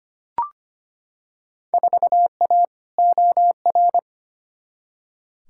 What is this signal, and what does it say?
Morse code, heard as a keyed tone, about 25 wpm, 700 Hz. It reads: 4A OR
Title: Field Day Exchanges Encore - 25wpm